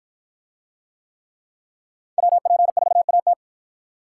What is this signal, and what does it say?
Morse code, heard as a keyed tone, about 53 wpm, 700 Hz. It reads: WP4AT